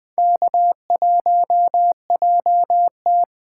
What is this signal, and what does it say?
Morse code, heard as a keyed tone, about 20 wpm, 700 Hz. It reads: K1JT
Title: USA Call Signs - 3x Character Spacing - 20wpm